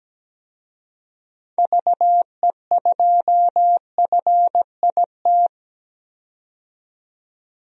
Morse code, heard as a keyed tone, about 17 wpm, 700 Hz. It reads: VE2FIT
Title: DXCC-100 Call Signs - 6 Characters - 17wpm